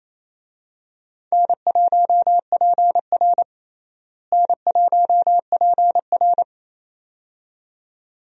Morse code, heard as a keyed tone, about 28 wpm, 700 Hz. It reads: N1PR N1PR
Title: USA Call Signs - 4 Characters - Repeated 2x - 28wpm